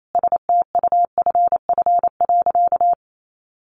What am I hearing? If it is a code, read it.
Morse code, heard as a keyed tone, about 28 wpm, 700 Hz. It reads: STUFF.
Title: Sentences from Top 1400 Words 28wpm